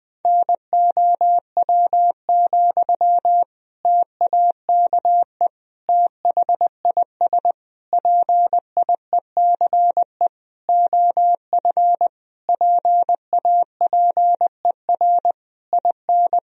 Morse code, heard as a keyed tone, about 21 wpm, 700 Hz. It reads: NOW, TAKE THIS PIECE OF PAPER IN
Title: Sentences from Top 400 Words and Abbreviations 20wpm